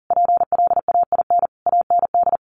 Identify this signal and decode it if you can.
Morse code, heard as a keyed tone, about 40 wpm, 700 Hz. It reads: PLAIN AND
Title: Sentences from Top 1400 Words 40wpm